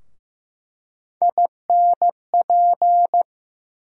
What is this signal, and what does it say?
Morse code, heard as a keyed tone, about 15 wpm, 700 Hz. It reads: INP
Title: USA Call Sign Postfixes - 3 Letters 15wpm